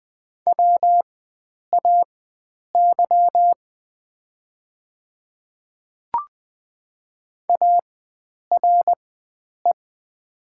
Morse code, heard as a keyed tone, about 20 wpm, spaced as for 9 wpm, 700 Hz. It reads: WAY ARE
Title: 4x Character Spacing - Top 100 Words - 20wpm